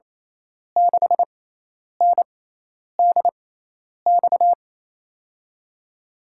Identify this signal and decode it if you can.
Morse code, heard as a keyed tone, about 28 wpm, spaced as for 10 wpm, 700 Hz. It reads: 6NDX
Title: USA Call Signs - 6x Character Spacing - 28wpm